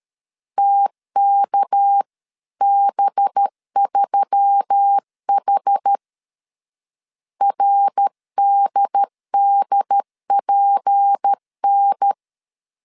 Morse code, heard as a keyed tone, about 13 wpm, 800 Hz. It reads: TKB3H RDDPN
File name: CW / DET / 1 6 1500C S A 12wpm 3440